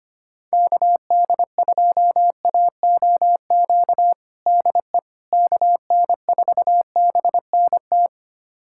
Morse code, heard as a keyed tone, about 25 wpm, 700 Hz. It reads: KD2AOQ DE KN4BNT
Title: USA Callsign DE Callsign - 6 Characters - 25wpm